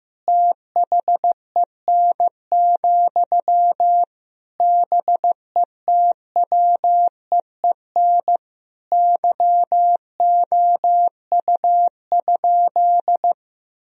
Morse code, heard as a keyed tone, about 15 wpm, 700 Hz. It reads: THEN, BETWEEN YOU?
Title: Sentences from Top 1000 Words - Review Part 4 - 15wpm